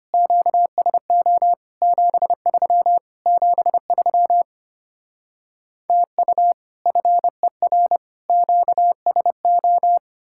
Morse code, heard as a keyed tone, about 33 wpm, 700 Hz. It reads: QSO 73 73 TU FER QSO